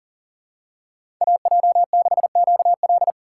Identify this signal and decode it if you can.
Morse code, heard as a keyed tone, about 40 wpm, 700 Hz. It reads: AJ6QL
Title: K1USN SST QSO Practice - 40wpm